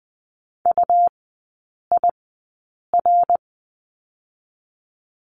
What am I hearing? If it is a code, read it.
Morse code, heard as a keyed tone, about 20 wpm, 700 Hz. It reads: U I R
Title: 121 Lesson 20e - Number 3 - Three Characters 20wpm